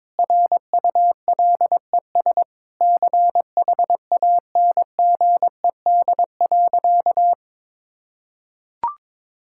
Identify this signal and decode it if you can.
Morse code, heard as a keyed tone, about 22 wpm, 700 Hz. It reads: RULES CHANGED.
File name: Sentences with 7-Word Phrases - 22wpm